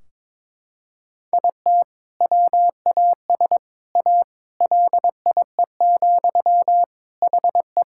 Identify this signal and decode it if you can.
Morse code, heard as a keyed tone, about 22 wpm, 700 Hz. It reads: IT WAS A LIE, HE
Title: Sentences from Top 1600 Words 22wpm